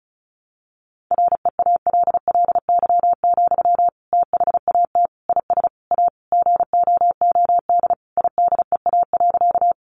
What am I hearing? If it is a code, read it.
Morse code, heard as a keyed tone, about 34 wpm, 700 Hz. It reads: REALLY, THAT IS A GOOD IDEA.
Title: Sentences from Top 600 Words 35wpm